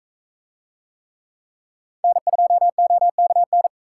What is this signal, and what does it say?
Morse code, heard as a keyed tone, about 44 wpm, 700 Hz. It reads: N2OKN